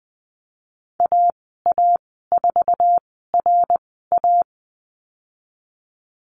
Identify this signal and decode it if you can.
Morse code, heard as a keyed tone, about 20 wpm, spaced as for 15 wpm, 700 Hz. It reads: AA4RA
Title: 076 Lesson 13g - Letter L - Call Signs 20wpm